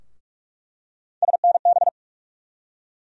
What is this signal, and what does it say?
Morse code, heard as a keyed tone, about 47 wpm, 700 Hz. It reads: SNB
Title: USA Call Sign Postfixes - 3 Letters 45wpm